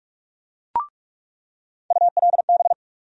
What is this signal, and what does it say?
Morse code, heard as a keyed tone, about 41 wpm, 700 Hz. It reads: ULB
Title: USA Call Sign Postfixes - 3 Letters 45wpm